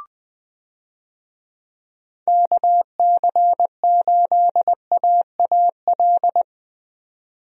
Morse code, heard as a keyed tone, about 20 wpm, 700 Hz. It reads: KC8AAL
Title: USA Call Signs - 6 Characters - 20wpm